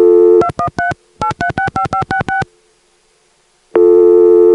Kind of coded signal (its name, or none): DTMF